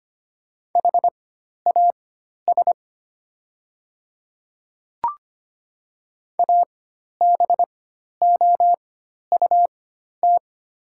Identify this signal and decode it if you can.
Morse code, heard as a keyed tone, about 25 wpm, spaced as for 12 wpm, 690 Hz. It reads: HAS ABOUT
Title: 4x Character Spacing - Top 200 Words - 25wpm